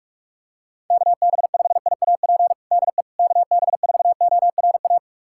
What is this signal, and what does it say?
Morse code, heard as a keyed tone, about 46 wpm, 700 Hz. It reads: KB5IAP DE KB4ORA
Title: USA Callsign DE Callsign - 6 Characters - 45wpm